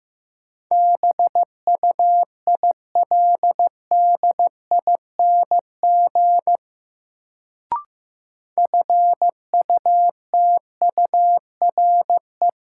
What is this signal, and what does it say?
Morse code, heard as a keyed tone, about 15 wpm, 700 Hz. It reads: BUILDING FUTURE
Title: Sentences from Top 700 Words - New Word Drill - 15wpm